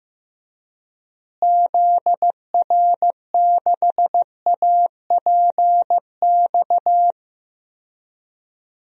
Morse code, heard as a keyed tone, about 15 wpm, 700 Hz. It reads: ZR6APX
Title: DXCC-100 Call Signs - 6 Characters - Repeated 2x - 15wpm